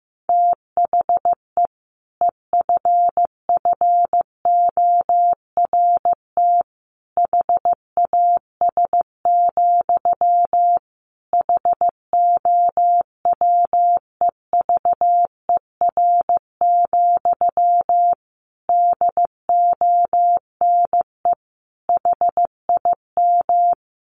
Morse code, heard as a keyed tone, about 14 wpm, 700 Hz. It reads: THE EFFORT HAS, HOWEVER, DONE HIM
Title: Sentences from Top 1200 Words 15wpm